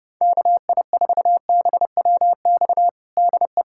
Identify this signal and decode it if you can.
Morse code, heard as a keyed tone, about 32 wpm, 700 Hz. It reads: KI4BWX DE
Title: USA Callsign DE Callsign - 6 Characters - 30wpm